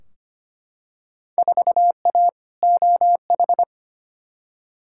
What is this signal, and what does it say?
Morse code, heard as a keyed tone, about 25 wpm, 700 Hz. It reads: 4A OH
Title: Field Day Exchanges - 25wpm